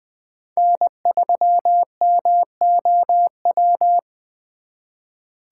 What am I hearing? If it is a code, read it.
Morse code, heard as a keyed tone, about 25 wpm, 700 Hz. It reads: N3MOW
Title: Speed Racing - USA 1x3 Call Signs 20wpm